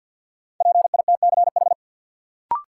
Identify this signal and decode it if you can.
Morse code, heard as a keyed tone, about 51 wpm, 700 Hz. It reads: PITCH
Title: Sentences from Top 1300 Words - New Word Drill Alt - 50wpm